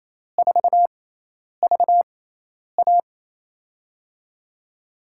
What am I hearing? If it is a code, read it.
Morse code, heard as a keyed tone, about 27 wpm, spaced as for 10 wpm, 700 Hz. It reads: 4VA
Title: USA Call Signs - 4 Characters - 6x Character Spacing - 28wpm